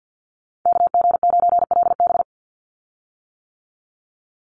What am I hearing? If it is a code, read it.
Morse code, heard as a keyed tone, about 50 wpm, 700 Hz. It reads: KG9LB
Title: Call Signs - 50wpm